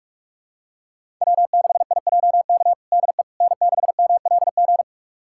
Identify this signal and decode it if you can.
Morse code, heard as a keyed tone, about 45 wpm, 700 Hz. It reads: W6IJK DE N6MLG
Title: USA Callsign DE Callsign - 5 Characters - 45wpm